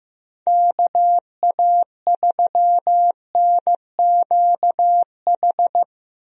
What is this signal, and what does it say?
Morse code, heard as a keyed tone, about 16 wpm, 700 Hz. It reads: KA3NQH
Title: USA Call Signs - 6 Characters - 15wpm